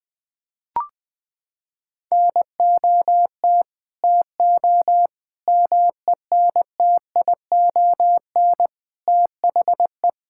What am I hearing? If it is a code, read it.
Morse code, heard as a keyed tone, about 20 wpm, 700 Hz. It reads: NOT TO MENTION THE
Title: Sentences with 6-Word Phrases - 20wpm